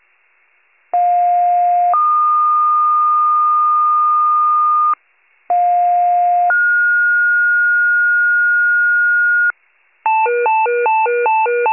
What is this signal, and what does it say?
Two-tone page, heard as a one-second tone, then a three-second tone, about 710 then 1220 Hz, about 710 then 1430 Hz, with noise. It is heard from 0.9 to 9.5 s.
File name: pager / 46 FIRE / wcalert